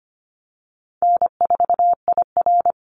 Morse code, heard as a keyed tone, about 25 wpm, 700 Hz. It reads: N4IR
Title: Call Signs Repeated 3x - 25wpm